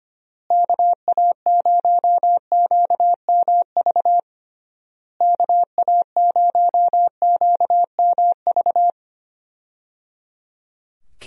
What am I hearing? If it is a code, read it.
Morse code, heard as a keyed tone, about 25 wpm, 700 Hz. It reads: KA0QMV KA0QMV
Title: USA Call Signs - 6 Characters - Repeated 3x - 25wpm